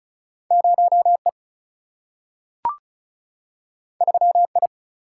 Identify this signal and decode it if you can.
Morse code, heard as a keyed tone, about 36 wpm, 700 Hz. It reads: 0E 3I
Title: Single-Letter Single-Number 35wpm